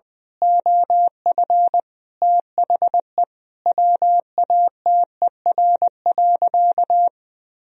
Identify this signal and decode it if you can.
Morse code, heard as a keyed tone, about 19 wpm, 700 Hz. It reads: OF THE WATER.
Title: Sentences from Top 1000 Words - Review Part 3 - 20wpm